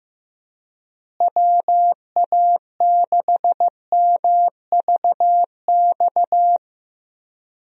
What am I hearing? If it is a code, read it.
Morse code, heard as a keyed tone, about 15 wpm, 700 Hz. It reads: WA6MVX
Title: USA Call Signs - 6 Characters - Repeated 2x - 15wpm